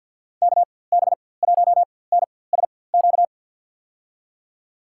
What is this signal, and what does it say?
Morse code, heard as a keyed tone, about 51 wpm, spaced as for 23 wpm, 700 Hz. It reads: KB1NSQ
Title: USA Call Signs - 6 Characters - 4x Character Spacing - 50wpm